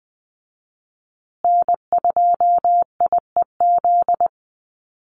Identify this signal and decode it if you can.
Morse code, heard as a keyed tone, about 20 wpm, 700 Hz. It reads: N2IEZ
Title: USA Call Signs Repeated 3x 20wpm